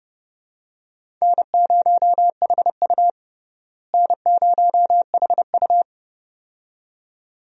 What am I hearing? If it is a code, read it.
Morse code, heard as a keyed tone, about 30 wpm, 700 Hz. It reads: N0HU N0HU
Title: USA Call Signs - 4 Characters - Repeated 2x - 30wpm